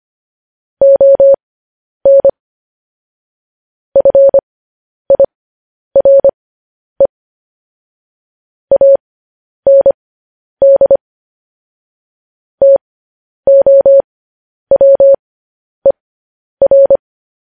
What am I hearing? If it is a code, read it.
Morse code, heard as a keyed tone, about 25 wpm, spaced as for 10 wpm, 550 Hz. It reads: ON FIRE AND TOWER
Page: news headlines in morse code at 10 wpm